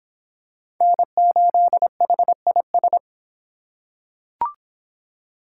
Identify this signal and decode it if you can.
Morse code, heard as a keyed tone, about 23 wpm, 700 Hz. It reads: N8HIS